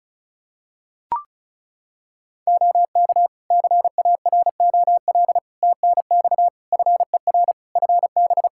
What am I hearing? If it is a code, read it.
Morse code, heard as a keyed tone, about 36 wpm, 700 Hz. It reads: OK CAROL TNX FER FB
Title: K1USN SST QSO Practice 35wpm